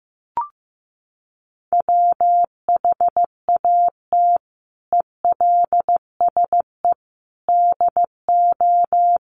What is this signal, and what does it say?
Morse code, heard as a keyed tone, about 15 wpm, 700 Hz. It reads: WHAT ELSE DO
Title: Sentences from Top 1100 Words 15wpm